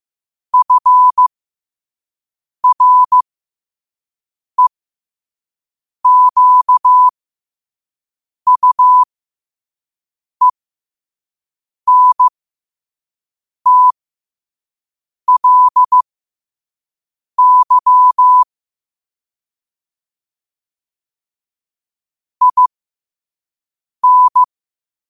Morse code, heard as a keyed tone, about 15 wpm, spaced as for 5 wpm, 1000 Hz. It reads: FREQUENTLY IN
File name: qotdfiles / 5wpm